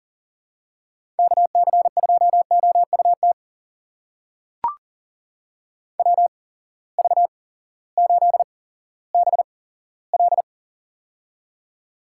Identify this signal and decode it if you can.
Morse code, heard as a keyed tone, about 40 wpm, spaced as for 18 wpm, 700 Hz. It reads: KC2OUT WV8BL
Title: USA Call Signs - 8x Character Spacing - 40wpm